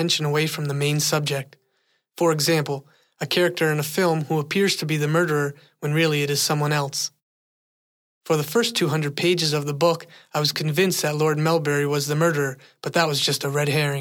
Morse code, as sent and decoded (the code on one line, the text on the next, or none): none